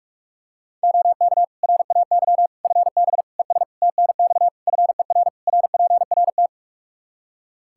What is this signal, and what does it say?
Morse code, heard as a keyed tone, about 46 wpm, 700 Hz. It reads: OK RAY FB ES TNX FER RPRT
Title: Practice QSO - Exchange 2 - Info - 45wpm